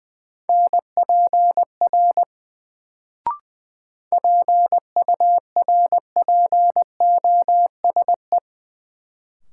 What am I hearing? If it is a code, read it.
Morse code, heard as a keyed tone, about 20 wpm, 700 Hz. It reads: NPR PURPOSE
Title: Sets of 5 Words - New Word Drill - 20wpm